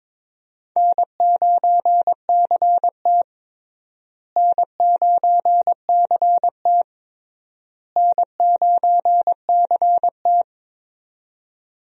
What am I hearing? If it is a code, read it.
Morse code, heard as a keyed tone, about 22 wpm, 700 Hz. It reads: N9CT N9CT N9CT
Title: USA Call Signs - 4 Characters - Repeated 3x - 22wpm